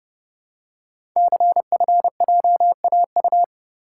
Morse code, heard as a keyed tone, about 30 wpm, 700 Hz. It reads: CFJAU
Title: Code Groups - 5 Letters - 30wpm